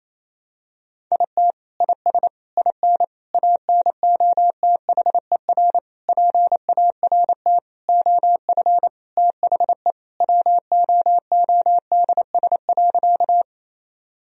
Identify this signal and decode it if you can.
Morse code, heard as a keyed tone, about 28 wpm, 700 Hz. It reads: IT IS IN ANOTHER PART OF THE WOODS.
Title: Sentences from Top 1000 Words - Review Part 2 - 28wpm